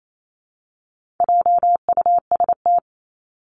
Morse code, heard as a keyed tone, about 28 wpm, 700 Hz. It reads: JUST
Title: Top 300 Words Repeated 28wpm